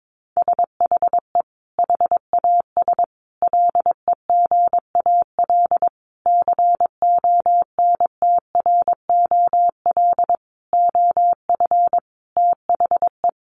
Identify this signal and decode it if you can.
Morse code, heard as a keyed tone, about 22 wpm, 700 Hz. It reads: SHE HAS LEGAL CONTROL OF THE
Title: Sentences from Top 1200 Words 22wpm